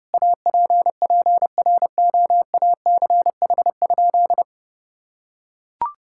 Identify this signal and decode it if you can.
Morse code, heard as a keyed tone, about 32 wpm, 700 Hz. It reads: APPROACH?
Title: Sentences from Top 1000 Words - Review Part 6 - 30wpm